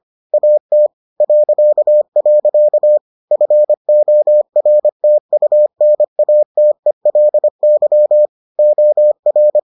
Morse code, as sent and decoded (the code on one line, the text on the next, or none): .- - / .-.-.- .-.-.- / ..-. --- .-. - ..- -. .- - . .-.. -.-- / --- .-.
AT .. FORTUNATELY OR